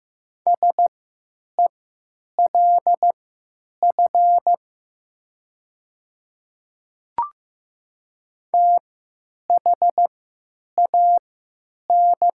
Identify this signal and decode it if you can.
Morse code, heard as a keyed tone, about 15 wpm, spaced as for 8 wpm, 700 Hz. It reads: SELF THAN
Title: 3x Character Spacing - Top 200 Words - 15wpm